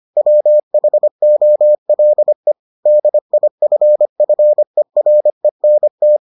Morse code, heard as a keyed tone, about 25 wpm, 600 Hz. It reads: WHOLE DIFFERENT